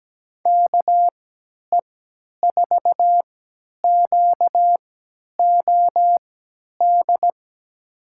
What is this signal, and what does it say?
Morse code, heard as a keyed tone, about 17 wpm, spaced as for 10 wpm, 700 Hz. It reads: KE4QOD
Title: USA Call Signs - 6 Characters - 3x Character Spacing - 17wpm